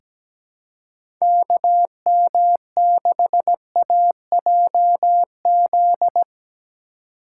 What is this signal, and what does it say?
Morse code, heard as a keyed tone, about 17 wpm, 700 Hz. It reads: KM6AJZ
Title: USA Call Signs - 6 Characters - 3x Character Spacing - 17wpm